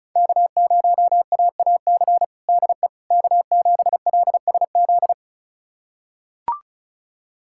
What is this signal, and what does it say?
Morse code, heard as a keyed tone, about 36 wpm, 700 Hz. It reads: K0AAC DE K7LSZ
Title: USA Callsign DE Callsign - 5 Characters - 35wpm